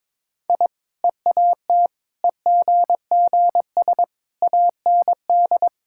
Morse code, heard as a keyed tone, about 22 wpm, 700 Hz. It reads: I EAT EGGS AND
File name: practice / 22wpm / Easy Sentences with Hints - 22wpm